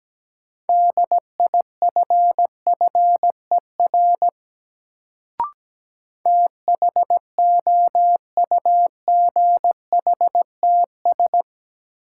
Morse code, heard as a keyed tone, about 17 wpm, 700 Hz. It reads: DIFFER THOUGHTS
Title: Sentences from Top 1000 Words - Review Part 1 - New Word Drill - 17wpm